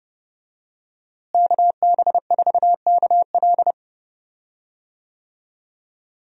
Morse code, heard as a keyed tone, about 31 wpm, 700 Hz. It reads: KB4KL
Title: USA Call Signs - 5 Characters - Repeated 3x - 30wpm